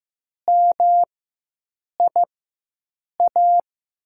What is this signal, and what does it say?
Morse code, heard as a keyed tone, about 16 wpm, spaced as for 7 wpm, 700 Hz. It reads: MIA
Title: USA Call Signs - 4x Character Spacing - 15wpm